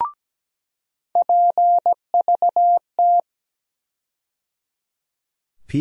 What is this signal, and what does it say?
Morse code, heard as a keyed tone, about 17 wpm, 700 Hz. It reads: PVT
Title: USA Call Sign Postfixes - 3 Letters 17wpm